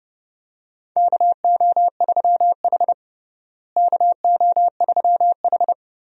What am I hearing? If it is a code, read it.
Morse code, heard as a keyed tone, about 30 wpm, 700 Hz. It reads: KO3H KO3H
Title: USA Call Signs - 4 Characters - Repeated 3x - 30wpm